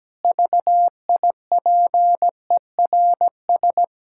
Morse code, heard as a keyed tone, about 17 wpm, 700 Hz. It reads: VIPERS
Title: Pangram Challenge v2 - 17wpm